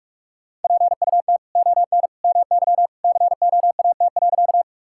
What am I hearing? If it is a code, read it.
Morse code, heard as a keyed tone, about 45 wpm, 700 Hz. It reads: PUT ON MY COAT.
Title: Sentences from Top 1600 Words 45wpm